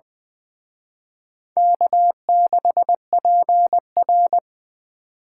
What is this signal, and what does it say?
Morse code, heard as a keyed tone, about 23 wpm, 700 Hz. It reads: K6PR
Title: Speed Racing - USA 1x2 Call Signs 20wpm